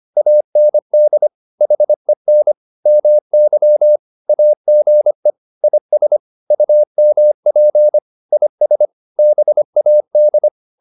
Morse code, heard as a keyed tone, about 25 wpm, 600 Hz. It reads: AND HEN MY AGE IS UMP IS BAD